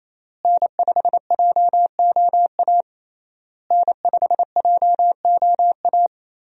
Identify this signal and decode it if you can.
Morse code, heard as a keyed tone, about 28 wpm, 700 Hz. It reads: N5JOA N5JOA
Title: USA Call Signs - 5 Characters - Repeated 3x - 28wpm